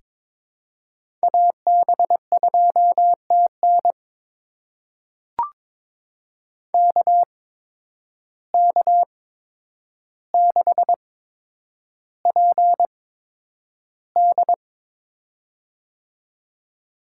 Morse code, heard as a keyed tone, about 22 wpm, spaced as for 9 wpm, 700 Hz. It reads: AB2TN KK6PD
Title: USA Call Signs - 5 Characters - 8x Character Spacing - 22wpm